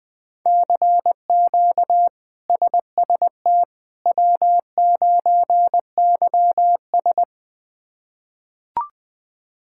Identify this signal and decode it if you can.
Morse code, heard as a keyed tone, about 21 wpm, 700 Hz. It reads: CQ SST W9YS
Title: K1USN SST QSO Practice - 20wpm